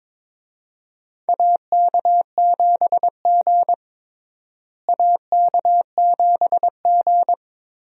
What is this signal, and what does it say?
Morse code, heard as a keyed tone, about 22 wpm, 700 Hz. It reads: AK7G AK7G
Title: USA Call Signs - 4 Characters - Repeated 2x - 22wpm